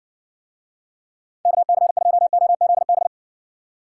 Morse code, heard as a keyed tone, about 56 wpm, 700 Hz. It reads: KC2OZB